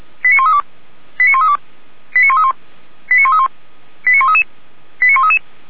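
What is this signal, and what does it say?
Five-tone selective call (ZVEI), heard as quick runs of five tones, tones of about 70 ms. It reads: 89132 89132 89131 89131 89133 89133
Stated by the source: ZVEI five-tone selective call 89200 received by SDR, from the audio archive of Feuerwehr Orlamünde.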